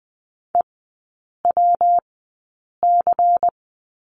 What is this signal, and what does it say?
Morse code, heard as a keyed tone, about 20 wpm, spaced as for 8 wpm, 700 Hz. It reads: EWC
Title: 114 Lesson 19e - Letter w - Three Characters 20wpm